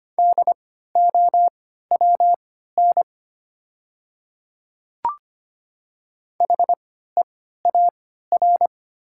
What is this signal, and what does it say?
Morse code, heard as a keyed tone, about 25 wpm, spaced as for 14 wpm, 700 Hz. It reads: DOWN HEAR
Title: 3x Character Spacing - Top 500 Words - 25wpm